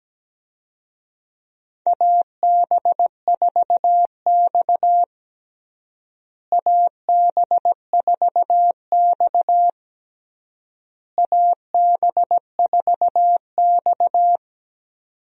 Morse code, heard as a keyed tone, about 17 wpm, 700 Hz. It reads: AB4X AB4X AB4X
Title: USA Call Signs - 4 Characters - Repeated 3x - 17wpm